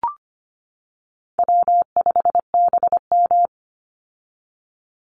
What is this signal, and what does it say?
Morse code, heard as a keyed tone, about 25 wpm, 700 Hz. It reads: W5BM
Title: Easy Call Signs - 25wpm